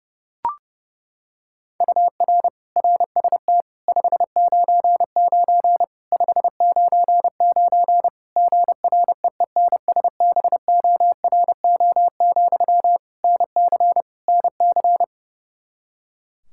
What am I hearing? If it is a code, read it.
Morse code, heard as a keyed tone, about 30 wpm, 700 Hz. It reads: UR RST 599 599 GREENSBORO, NC NC
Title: K1USN SST QSO Practice 30wpm